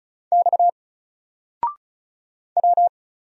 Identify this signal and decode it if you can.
Morse code, heard as a keyed tone, about 36 wpm, 700 Hz. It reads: X W
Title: Single Character (Letter:Number:Punctuation:Prosign) 35wpm